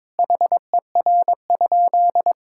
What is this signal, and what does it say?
Morse code, heard as a keyed tone, about 21 wpm, 700 Hz. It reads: HER?
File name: practice / 22wpm / Sentences with 2-Word Phrases - 22wpm